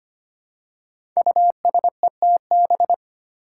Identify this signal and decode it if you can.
Morse code, heard as a keyed tone, about 25 wpm, 700 Hz. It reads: USETB
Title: Five Challenging Letter-Number Combinations 25wpm